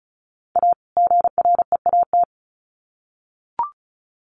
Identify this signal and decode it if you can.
Morse code, heard as a keyed tone, about 34 wpm, 700 Hz. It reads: A GREAT
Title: Sets of 2 Words 35wpm